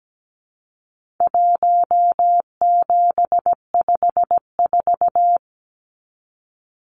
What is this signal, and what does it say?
Morse code, heard as a keyed tone, about 17 wpm, 700 Hz. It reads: 1754
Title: Numbers - Contest Style - 4-Digit 17wpm